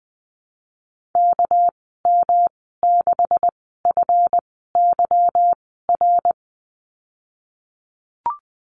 Morse code, heard as a keyed tone, about 20 wpm, spaced as for 14 wpm, 700 Hz. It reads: KM6FYR
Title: 238 Lesson 38g - Letter q - Call Signs 20wpm